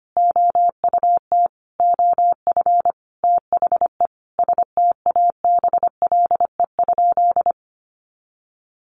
Morse code, heard as a keyed tone, about 25 wpm, 700 Hz. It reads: OUT OF THE STABLE?
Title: Sentences from Top 1500 Words 25wpm